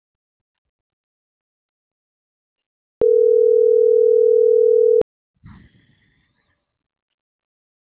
Ringback tone (North America), 440 + 480 Hz.
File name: CENTRAL MAYAB / out marcela.ramos - .3369